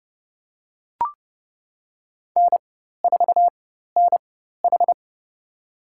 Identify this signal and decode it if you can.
Morse code, heard as a keyed tone, about 31 wpm, spaced as for 14 wpm, 700 Hz. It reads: N4NH
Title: USA Call Signs - 4 Characters - 4x Character Spacing - 30wpm